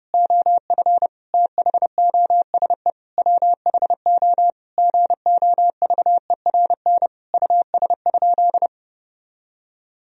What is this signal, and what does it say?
Morse code, heard as a keyed tone, about 32 wpm, 700 Hz. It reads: OF THOSE WHO GOVERN US?
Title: Sentences from Top 1000 Words - Review Part 2 - 30wpm